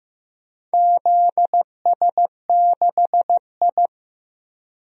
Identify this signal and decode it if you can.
Morse code, heard as a keyed tone, about 15 wpm, 700 Hz. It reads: ZS6I
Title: DXCC-100 Call Signs - 4 Characters - Repeated 3x - 15wpm